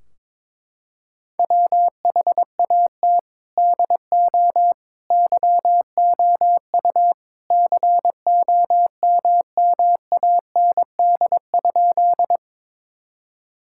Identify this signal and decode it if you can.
Morse code, heard as a keyed tone, about 22 wpm, 700 Hz. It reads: WHAT DO YOU COMMAND?
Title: Sentences from Top 1600 Words 22wpm